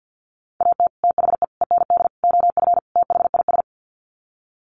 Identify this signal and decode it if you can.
Morse code, heard as a keyed tone, about 50 wpm, 700 Hz. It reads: AT THE END OF THIS
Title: Sets of 5 Words 50wpm